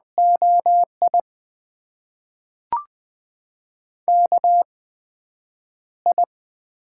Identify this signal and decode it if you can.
Morse code, heard as a keyed tone, about 20 wpm, spaced as for 5 wpm, 700 Hz. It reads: OIKI
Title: USA Call Signs - 6 Characters - 8x Character Spacing - 20wpm